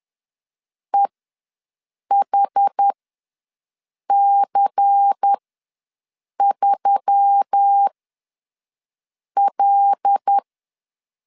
Morse code, heard as a keyed tone, about 10 wpm, spaced as for 5 wpm, 800 Hz. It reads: EHC3L
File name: LEZIONI / 6 PKX0 / 6 PK ws 10wpm 3745